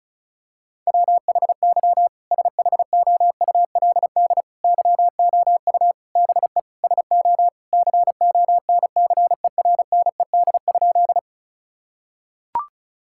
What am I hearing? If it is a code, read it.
Morse code, heard as a keyed tone, about 36 wpm, 700 Hz. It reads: WHY SHOULD YOU BE SO CONCERNED?
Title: Sentences from Top 1000 Words - Review Part 6 - 35wpm